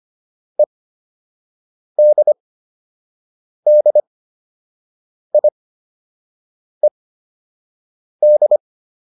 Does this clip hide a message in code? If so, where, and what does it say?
Yes: Morse code from 0.6 to 8.6 s, heard as a keyed tone, about 25 wpm, spaced as for 6 wpm, 600 Hz. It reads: EDDIED